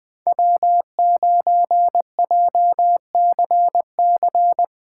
Morse code, heard as a keyed tone, about 20 wpm, 700 Hz. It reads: W9JCC
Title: USA Callsign DE Callsign - 5 Characters - 20wpm